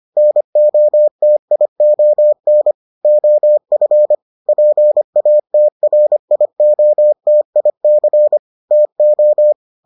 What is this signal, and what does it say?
Morse code, heard as a keyed tone, about 25 wpm, 600 Hz. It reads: NOTION OF PATRIOTIC TO